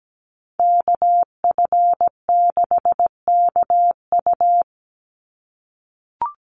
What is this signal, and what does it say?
Morse code, heard as a keyed tone, about 17 wpm, 700 Hz. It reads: KF6KU
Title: USA Call Signs 17wpm